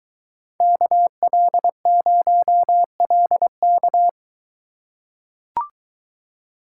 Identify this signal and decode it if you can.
Morse code, heard as a keyed tone, about 23 wpm, 700 Hz. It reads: KL0LK